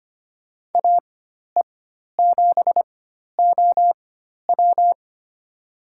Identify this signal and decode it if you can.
Morse code, heard as a keyed tone, about 25 wpm, spaced as for 12 wpm, 700 Hz. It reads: AE7OW
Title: USA Call Signs - 5 Characters - 4x Character Spacing - 25wpm